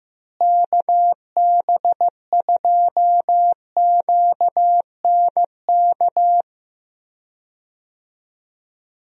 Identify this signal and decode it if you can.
Morse code, heard as a keyed tone, about 15 wpm, 700 Hz. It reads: KB2QNK
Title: USA Call Signs - 6 Characters - Repeated 3x - 15wpm